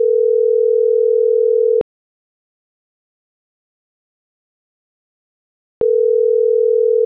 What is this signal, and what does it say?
Ringback tone (North America), 440 + 480 Hz.